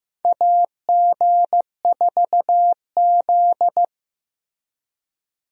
Morse code, heard as a keyed tone, about 15 wpm, 700 Hz. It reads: AG4Z
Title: USA Call Signs - 4 Characters - Repeated 2x - 15wpm